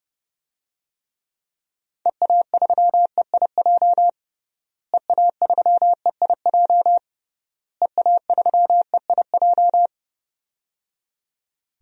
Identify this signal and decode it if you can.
Morse code, heard as a keyed tone, about 30 wpm, 700 Hz. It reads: EA3EIJ EA3EIJ EA3EIJ
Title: DXCC-100 Call Signs - 6 Characters - Repeated 3x - 30wpm